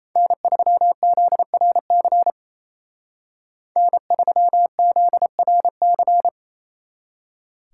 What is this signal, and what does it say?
Morse code, heard as a keyed tone, about 35 wpm, 700 Hz. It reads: N3ZRC N3ZRC
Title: Speed Racing - USA 1x3 Call Signs 28wpm